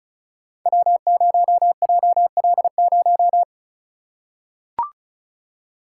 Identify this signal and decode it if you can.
Morse code, heard as a keyed tone, about 36 wpm, 700 Hz. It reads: W0JL0
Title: Code Groups - 5 Letter:Numbers - 35wpm